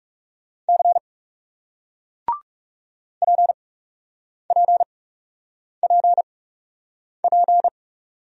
Morse code, heard as a keyed tone, about 40 wpm, spaced as for 8 wpm, 700 Hz. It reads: C PPPP